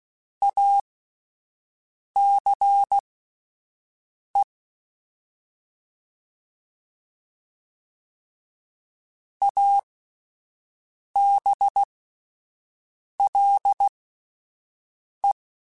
Morse code, heard as a keyed tone, about 16 wpm, spaced as for 5 wpm, 800 Hz. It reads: ACE ABLE